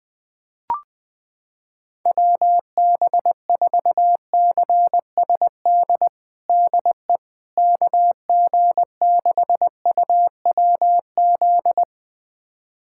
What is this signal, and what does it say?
Morse code, heard as a keyed tone, about 20 wpm, 700 Hz. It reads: WB4CSD DE KG6UWZ
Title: USA Callsign DE Callsign 20wpm